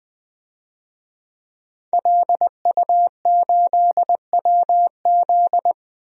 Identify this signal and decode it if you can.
Morse code, heard as a keyed tone, about 20 wpm, 700 Hz. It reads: LU8WZ
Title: DXCC-100 Call Signs - 5 Characters - Repeated 3x - 20wpm